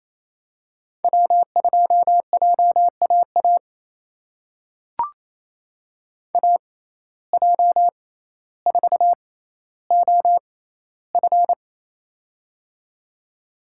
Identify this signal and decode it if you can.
Morse code, heard as a keyed tone, about 28 wpm, spaced as for 14 wpm, 700 Hz. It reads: W2JAA AJ4OF
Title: USA Call Signs - 5 Characters - 6x Character Spacing - 28wpm